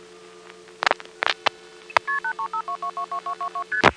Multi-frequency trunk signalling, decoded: KP8352222222ST